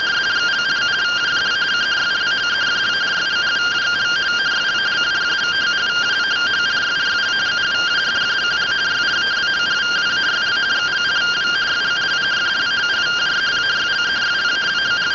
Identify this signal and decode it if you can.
Two-tone teleprinter signal, about 170 Hz shift.